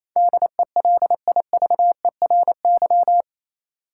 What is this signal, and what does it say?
Morse code, heard as a keyed tone, about 28 wpm, 700 Hz. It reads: DELIVERY
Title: Sentences from Top 1500 Words - New Word Drill Alt - 28wpm